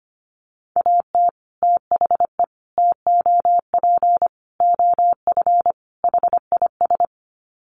Morse code, heard as a keyed tone, about 25 wpm, 700 Hz. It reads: AT THE TOP OF HIS
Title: Sets of 5 Words Encore 25wpm